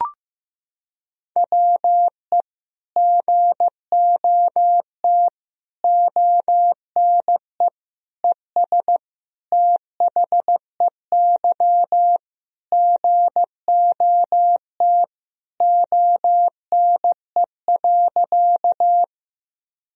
Morse code, heard as a keyed tone, about 15 wpm, 700 Hz. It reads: WE GOT ONE ES THEY GOT ONE.
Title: Sentences from Top 300 Words and Abbreviations 15wpm